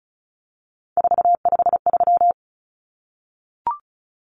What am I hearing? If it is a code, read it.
Morse code, heard as a keyed tone, about 34 wpm, 700 Hz. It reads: 453
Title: Common Cut Numbers - 3-Digit 35wpm